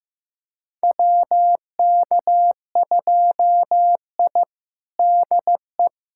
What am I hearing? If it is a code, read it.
Morse code, heard as a keyed tone, about 16 wpm, 700 Hz. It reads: WK2I DE
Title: USA Callsign DE Callsign - 4 Characters - 15wpm